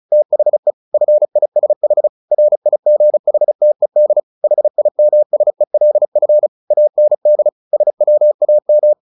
Morse code, heard as a keyed tone, about 34 wpm, 600 Hz. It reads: THE FISH RIGHTED HIMSELF AND SWAM
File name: the old man and the sea 2 35wpm